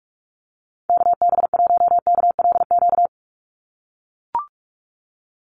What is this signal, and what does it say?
Morse code, heard as a keyed tone, about 44 wpm, 700 Hz. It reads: KB1KLQ
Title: USA Call Signs 45wpm